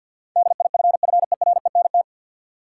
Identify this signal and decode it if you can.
Morse code, heard as a keyed tone, about 53 wpm, 700 Hz. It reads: DIFFERENT
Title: Sentences from Top 600 Words - New Word Drill - 50wpm